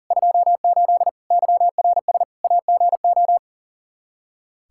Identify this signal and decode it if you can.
Morse code, heard as a keyed tone, about 40 wpm, 700 Hz. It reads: 28 YRS AGO
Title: Practice QSO - Exchange 3 - More Info - 40wpm